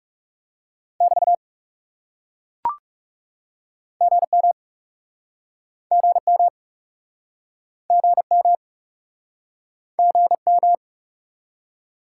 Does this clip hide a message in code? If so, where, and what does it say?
Yes: Morse code from 1.0 to 10.7 s, heard as a keyed tone, about 42 wpm, 700 Hz. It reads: <BT> GM GM GM GM